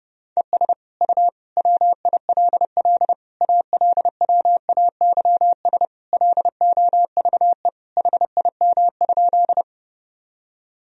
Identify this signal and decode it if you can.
Morse code, heard as a keyed tone, about 32 wpm, 700 Hz. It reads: ES U WILL ALWAYS LOVE HIM?
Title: Sentences from Top 400 Words and Abbreviations 30wpm